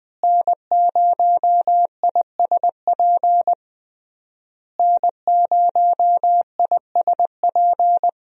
Morse code, heard as a keyed tone, about 20 wpm, 700 Hz. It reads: N0ISP N0ISP
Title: USA Call Signs - 5 Characters - Repeated 3x - 20wpm